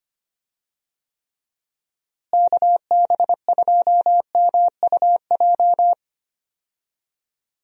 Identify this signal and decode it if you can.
Morse code, heard as a keyed tone, about 25 wpm, 700 Hz. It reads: KB2MUJ
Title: USA Call Signs - 6 Characters - Repeated 3x - 25wpm